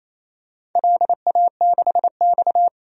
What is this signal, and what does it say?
Morse code, heard as a keyed tone, about 28 wpm, 700 Hz. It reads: LA6X
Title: DXCC-100 Call Signs - 4 Characters - 28wpm